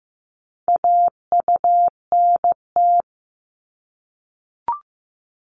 Morse code, heard as a keyed tone, about 15 wpm, 690 Hz. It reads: AUNT
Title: Multi-letter Phonemes 15wpm